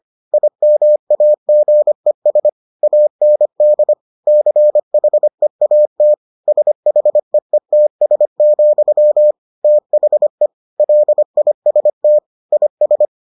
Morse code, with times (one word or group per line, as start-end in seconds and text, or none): 0.3-2.5 IMAGES
2.8-3.9 AND
4.3-6.1 CHEAT
6.5-9.3 SHEETS,
9.6-10.5 THE
10.8-12.2 LIST
12.5-13.1 IS